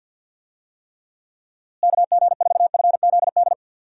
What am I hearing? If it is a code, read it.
Morse code, heard as a keyed tone, about 47 wpm, 700 Hz. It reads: KG4FZD
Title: USA Call Signs - 6 Characters - Repeated 2x - 50wpm